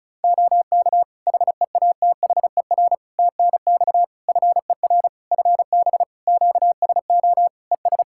Morse code, heard as a keyed tone, about 34 wpm, 700 Hz. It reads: OK HEATHER TNX FER FB QSO ES
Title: K1USN SST QSO Practice 35wpm